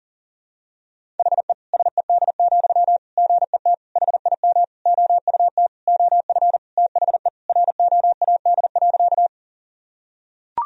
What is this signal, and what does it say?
Morse code, heard as a keyed tone, about 40 wpm, 700 Hz. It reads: HE SED, GET HIM OUT OF THE ROAD.
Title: Sentences from Top 400 Words and Abbreviations 40wpm